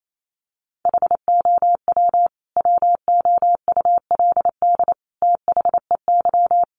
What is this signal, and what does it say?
Morse code, heard as a keyed tone, about 28 wpm, 700 Hz. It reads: HOW WOULD THEY
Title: Sentences from Top 300 Words 28wpm